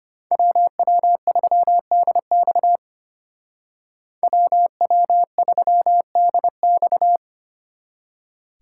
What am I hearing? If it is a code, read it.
Morse code, heard as a keyed tone, about 33 wpm, 700 Hz. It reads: WW3DX WW3DX